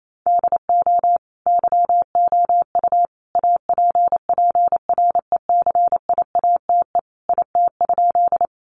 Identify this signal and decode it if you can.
Morse code, heard as a keyed tone, about 28 wpm, 700 Hz. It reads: DO YOU APPRECIATE IT?
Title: Sentences from Top 1300 Words 28wpm